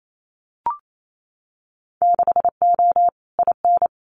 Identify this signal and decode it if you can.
Morse code, heard as a keyed tone, about 28 wpm, 700 Hz. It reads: 6O IN
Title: Winter Field Day Exchanges Encore 28wpm